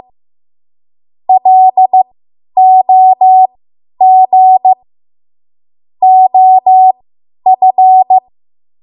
Morse code, heard as a keyed tone, about 15 wpm, spaced as for 10 wpm, 750 Hz. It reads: LOG OF